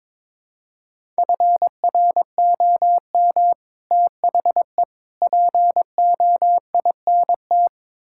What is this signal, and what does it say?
Morse code, heard as a keyed tone, about 21 wpm, 700 Hz. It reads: FROM THE POINT
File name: practice / 22wpm / Build A Phrase - 5-Word Phrases - 22wpm